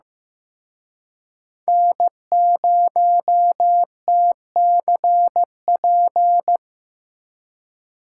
Morse code, heard as a keyed tone, about 18 wpm, 700 Hz. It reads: N0TCP
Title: Speed Racing - USA 1x3 Call Signs 15wpm